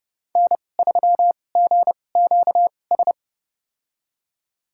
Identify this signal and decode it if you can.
Morse code, heard as a keyed tone, about 30 wpm, spaced as for 22 wpm, 700 Hz. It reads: N3GQS
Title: USA Call Signs - 5 Characters - 2x Character Spacing - 30wpm